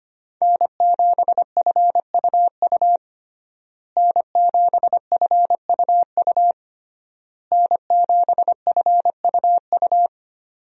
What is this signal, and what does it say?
Morse code, heard as a keyed tone, about 25 wpm, 700 Hz. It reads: N7FUU N7FUU N7FUU
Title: USA Call Signs - 5 Characters - Repeated 3x - 25wpm